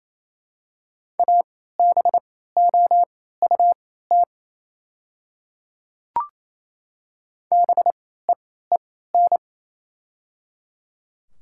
Morse code, heard as a keyed tone, about 27 wpm, spaced as for 16 wpm, 690 Hz. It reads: ABOUT BEEN
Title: 3x Character Spacing - Top 100 Words - 28wpm